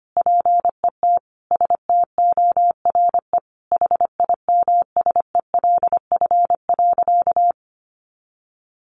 Morse code, heard as a keyed tone, about 25 wpm, 700 Hz. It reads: PET STORE HIMSELF.
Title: Additional Easy Phrases 25wpm